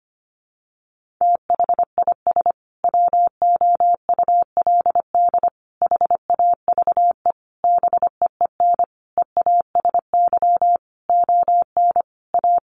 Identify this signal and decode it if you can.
Morse code, heard as a keyed tone, about 25 wpm, 700 Hz. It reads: THIS WOULD HAVE BEEN EASY ON A
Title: Sentences from Top 800 Words 25wpm